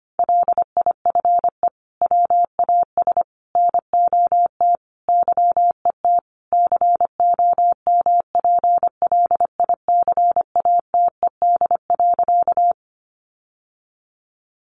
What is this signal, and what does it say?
Morse code, heard as a keyed tone, about 25 wpm, 700 Hz. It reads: LIFE WAS NOT YET COMPLICATED.
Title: Sentences from Top 1200 Words 25wpm